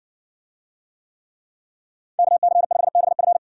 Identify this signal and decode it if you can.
Morse code, heard as a keyed tone, about 60 wpm, 700 Hz. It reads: KC5BF